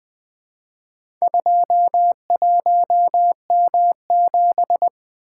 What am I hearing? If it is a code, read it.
Morse code, heard as a keyed tone, about 20 wpm, 700 Hz. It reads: 21M7
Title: Four Letter-Number Combinations - 20wpm